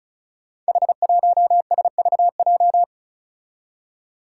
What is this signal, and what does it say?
Morse code, heard as a keyed tone, about 34 wpm, 700 Hz. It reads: H1SVJ
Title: Five Challenging Letter-Number Combinations 35wpm